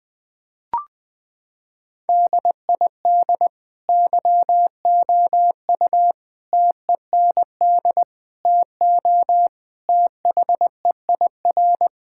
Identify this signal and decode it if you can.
Morse code, heard as a keyed tone, about 20 wpm, 700 Hz. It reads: DID YOU TEND TO THEIR
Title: Sentences from Top 1000 Words - Review Part 5 - 20wpm